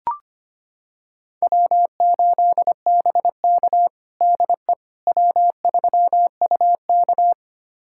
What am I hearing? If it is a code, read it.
Morse code, heard as a keyed tone, about 25 wpm, 700 Hz. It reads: W8BK DE W3UK
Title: USA Callsign DE Callsign - 4 Characters - 25wpm